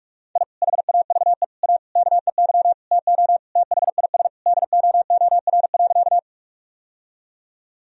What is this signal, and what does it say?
Morse code, heard as a keyed tone, about 47 wpm, 700 Hz. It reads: I HAVE A KEY TO THIS DOOR.
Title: Sentences from Top 1000 Words - Review Part 4 - 45wpm